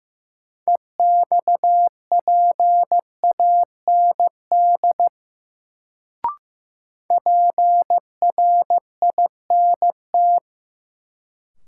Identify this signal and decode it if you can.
Morse code, heard as a keyed tone, about 15 wpm, 700 Hz. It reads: EXPAND PRINT
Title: Sentences from Top 1400 Words - New Word Drill - 15wpm